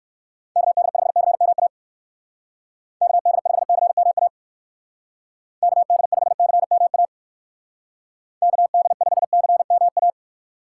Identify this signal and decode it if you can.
Morse code, heard as a keyed tone, about 54 wpm, 700 Hz. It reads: KD5CMA KD5CMA KD5CMA KD5CMA
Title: Speed Racing - USA 2x3 Call Signs 45wpm